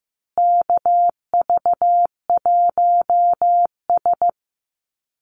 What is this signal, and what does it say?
Morse code, heard as a keyed tone, about 15 wpm, 700 Hz. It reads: KV1S
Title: USA Call Signs Repeated 2x 15wpm